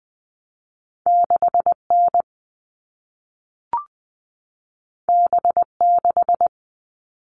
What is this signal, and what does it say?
Morse code, heard as a keyed tone, about 20 wpm, 700 Hz. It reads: 6N B6
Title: All Cut Numbers - 2-Digit 20wpm